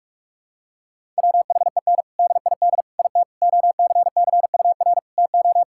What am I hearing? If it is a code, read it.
Morse code, heard as a keyed tone, about 47 wpm, 700 Hz. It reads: WHEN DID IT OCCUR TO
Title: Sentences from Top 1000 Words - Review Part 6 - 45wpm